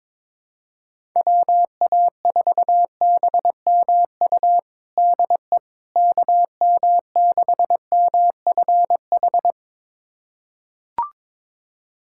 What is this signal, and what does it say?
Morse code, heard as a keyed tone, about 22 wpm, 700 Hz. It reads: WA4BMU DE KM6MFH
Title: USA Callsign DE Callsign 22wpm